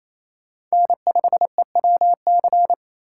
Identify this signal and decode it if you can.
Morse code, heard as a keyed tone, about 34 wpm, 700 Hz. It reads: N5EWC
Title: Speed Racing - USA 1x3 Call Signs 28wpm